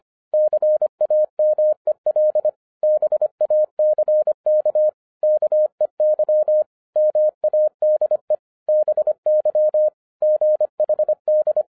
Morse code, heard as a keyed tone, about 25 wpm, 610 Hz. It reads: CAMEL BACK KEY MADE BY GHD
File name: 2011-10-mp3 / n14-MORSE EXPRESS 2011 CHRISTMAS KEY 25wpm